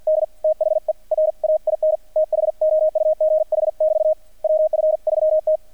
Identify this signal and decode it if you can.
Morse code, heard as a keyed tone, about 59 wpm, spaced as for 49 wpm, 610 Hz. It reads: D THE RAIN THOUGH<BT> WF3T
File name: archived / CRSnet